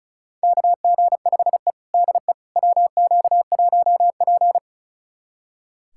Morse code, heard as a keyed tone, about 34 wpm, 700 Hz. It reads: KG5E DE WQ1P
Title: USA Callsign DE Callsign - 4 Characters - 35wpm